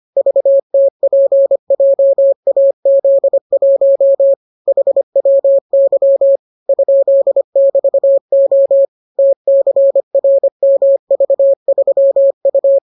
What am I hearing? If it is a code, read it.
Morse code, heard as a keyed tone, about 25 wpm, 550 Hz. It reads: VTPJAZ1 HWY ?<BT>O TCRMV3U